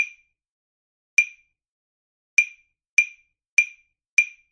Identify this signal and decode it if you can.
There is no code.